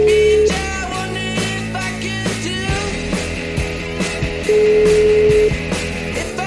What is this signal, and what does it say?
Ringback tone (Europe, 425 Hz), with noise.